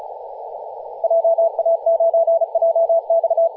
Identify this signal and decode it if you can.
Morse code, heard as a keyed tone, about 35 wpm, 660 Hz, with radio noise. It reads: JA9JX
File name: posts / 2016-nov-11-2050z-tl8ao-10-112mhz-cw